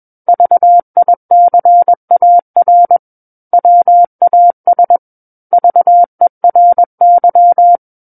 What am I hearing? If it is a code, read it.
Morse code, heard as a keyed tone, about 21 wpm, 700 Hz. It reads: VICAR WAS VERY
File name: Book One / 12-Chapter 12 - 21 WPM